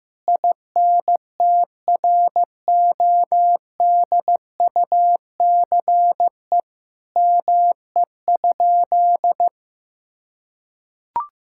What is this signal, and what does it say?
Morse code, heard as a keyed tone, about 15 wpm, 700 Hz. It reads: INTRODUCE ME?
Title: Sentences from Top 1600 Words 15wpm